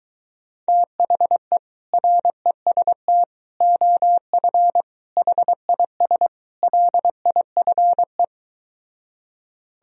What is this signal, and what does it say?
Morse code, heard as a keyed tone, about 23 wpm, 700 Hz. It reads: THE REST OF HIS LIFE